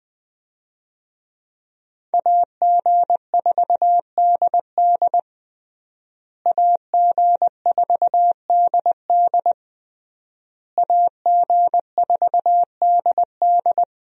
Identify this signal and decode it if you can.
Morse code, heard as a keyed tone, about 20 wpm, 700 Hz. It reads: AG4DD AG4DD AG4DD
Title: USA Call Signs - 5 Characters - Repeated 3x - 20wpm